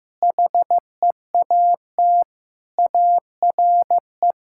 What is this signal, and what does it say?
Morse code, heard as a keyed tone, about 15 wpm, 700 Hz. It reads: HEAT ARE
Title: Sentences from Top 1000 Words - Review Part 5 - 15wpm